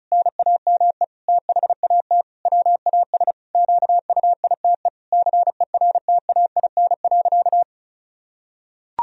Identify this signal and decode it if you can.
Morse code, heard as a keyed tone, about 36 wpm, 700 Hz. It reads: NAME THAT WAS QUITE CERTAIN.
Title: Sentences from Top 1000 Words - Review Part 2 - 35wpm